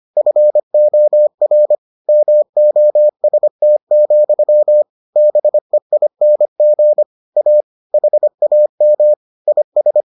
Morse code, heard as a keyed tone, about 25 wpm, 600 Hz. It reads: FOR MOST, BEING A HAM IS